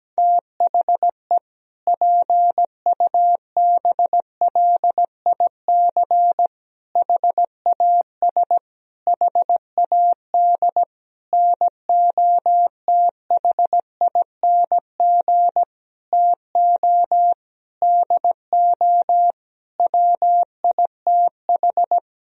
Morse code, heard as a keyed tone, about 17 wpm, 700 Hz. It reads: THE PUBLIC HAS HAD NOTHING TO DO WITH
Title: Sentences from Top 1000 Words - Review Part 3 - 17wpm